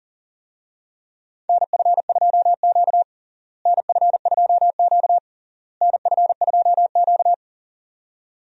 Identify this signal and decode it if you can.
Morse code, heard as a keyed tone, about 40 wpm, 700 Hz. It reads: NF2Q NF2Q NF2Q
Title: USA Call Signs - 4 Characters - Repeated 3x - 40wpm